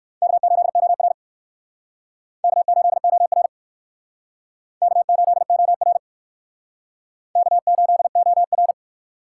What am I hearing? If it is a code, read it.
Morse code, heard as a keyed tone, about 59 wpm, 700 Hz. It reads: K8OR K8OR K8OR K8OR